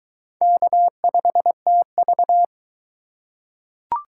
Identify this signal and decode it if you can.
Morse code, heard as a keyed tone, about 20 wpm, 700 Hz. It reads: K5TV